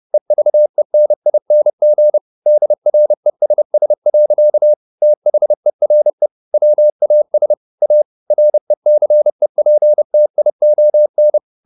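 Morse code, heard as a keyed tone, about 30 wpm, 600 Hz. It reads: EVENING DRESS. THERE WAS A RECEPTION
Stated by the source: Morse code at 30 wpm, the novel Never by Ken Follett in English, from CW med Gnister.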